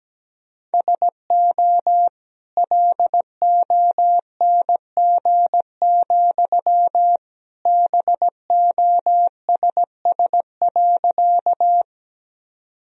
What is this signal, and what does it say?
Morse code, heard as a keyed tone, about 17 wpm, 700 Hz. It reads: SO LONG, BOSS.
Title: Sentences from Top 1000 Words - Review Part 4 - 17wpm